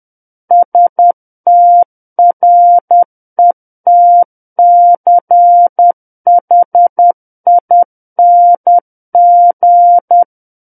Morse code, heard as a keyed tone, about 10 wpm, 700 Hz. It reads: STRETCHING
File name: Book One / 01-Chapter 1 - 10 WPM